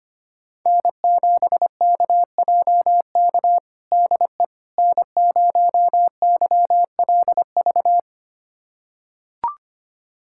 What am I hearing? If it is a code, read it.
Morse code, heard as a keyed tone, about 25 wpm, 700 Hz. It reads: N7KJK DE N0YLV
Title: USA Callsign DE Callsign - 5 Characters - 25wpm